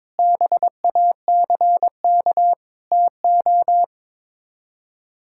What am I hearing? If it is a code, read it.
Morse code, heard as a keyed tone, about 22 wpm, 700 Hz. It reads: BACK TO
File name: practice / 22wpm / Sets of 8 Words - 22wpm